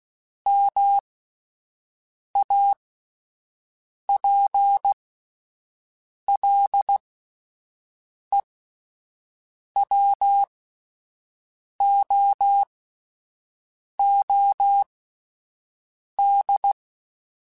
Morse code, heard as a keyed tone, about 16 wpm, spaced as for 5 wpm, 800 Hz. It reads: MAPLEWOOD